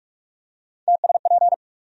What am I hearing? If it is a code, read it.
Morse code, heard as a keyed tone, about 46 wpm, 700 Hz. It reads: TSP
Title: USA Call Sign Postfixes - 3 Letters 45wpm